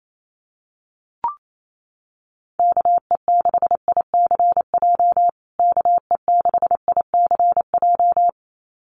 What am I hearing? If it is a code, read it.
Morse code, heard as a keyed tone, about 28 wpm, 700 Hz. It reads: KE6ICJ KE6ICJ
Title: USA Call Signs Repeated 2x 28wpm